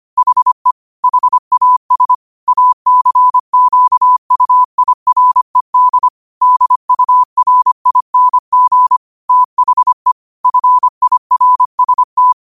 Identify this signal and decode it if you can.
Morse code, heard as a keyed tone, about 25 wpm, 1000 Hz. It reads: HE HAS ACQUIRED DURING THE FIRST